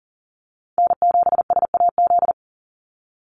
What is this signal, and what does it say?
Morse code, heard as a keyed tone, about 40 wpm, 700 Hz. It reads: N7SAZ
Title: USA Call Signs 40wpm